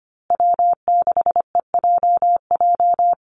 Morse code, heard as a keyed tone, about 25 wpm, 700 Hz. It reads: W6EJJ
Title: USA Call Signs Repeated 3x 25wpm